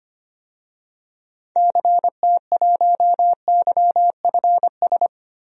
Morse code, heard as a keyed tone, about 25 wpm, 700 Hz. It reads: CT1YFS
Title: DXCC-100 Call Signs - 6 Characters - Repeated 3x - 25wpm